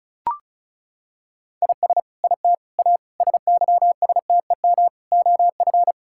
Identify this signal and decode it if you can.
Morse code, heard as a keyed tone, about 36 wpm, 700 Hz. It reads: IS IT A SYSTEM OF
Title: Sentences from Top 1000 Words - Review Part 6 - 35wpm